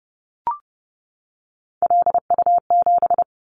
Morse code, heard as a keyed tone, about 30 wpm, 700 Hz. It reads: LU7
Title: Callsign Prefixes for Top 50 DXCC - 30wpm